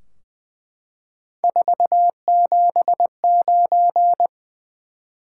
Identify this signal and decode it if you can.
Morse code, heard as a keyed tone, about 20 wpm, 700 Hz. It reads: 479
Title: Top 180 QSO Elements 20wpm